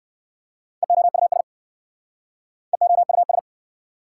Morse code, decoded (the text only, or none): EOUS EOUS